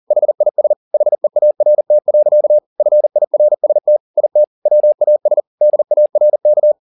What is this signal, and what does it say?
Morse code, heard as a keyed tone, about 40 wpm, 600 Hz. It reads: HIS HEART. FIRST IT WAS DARK
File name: the old man and the sea 2 40wpm